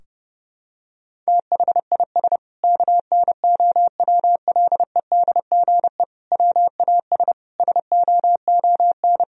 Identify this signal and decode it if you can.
Morse code, heard as a keyed tone, about 30 wpm, 700 Hz. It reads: THIS KNOWLEDGE WAS SOON
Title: Sentences from Top 1000 Words - Review Part 3 - 30wpm